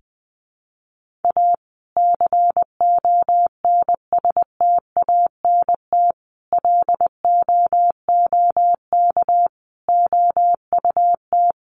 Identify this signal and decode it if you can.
Morse code, heard as a keyed tone, about 20 wpm, 700 Hz. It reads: A CONSTANT LOOK OUT
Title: Sentences from Top 1500 Words 20wpm